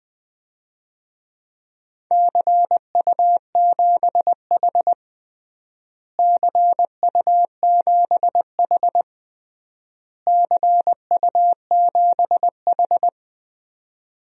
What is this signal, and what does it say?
Morse code, heard as a keyed tone, about 20 wpm, 700 Hz. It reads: CU7H CU7H CU7H
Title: DXCC-100 Call Signs - 4 Characters - Repeated 3x - 20wpm